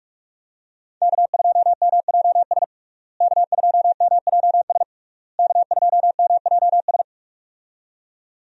Morse code, heard as a keyed tone, about 46 wpm, 700 Hz. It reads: K2MJS K2MJS K2MJS
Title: USA Call Signs - 5 Characters - Repeated 3x - 45wpm